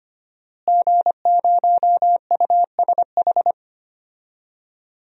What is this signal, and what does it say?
Morse code, heard as a keyed tone, about 25 wpm, 700 Hz. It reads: G0USH
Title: DXCC-100 Call Signs - 5 Characters - Repeated 2x - 25wpm